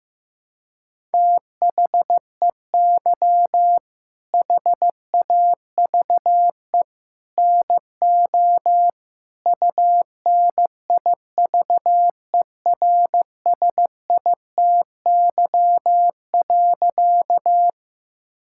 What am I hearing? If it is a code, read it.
Morse code, heard as a keyed tone, about 15 wpm, 700 Hz. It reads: THEY HAVE NO UNIVERSITY.
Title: Sentences from Top 1000 Words - Review Part 5 - 15wpm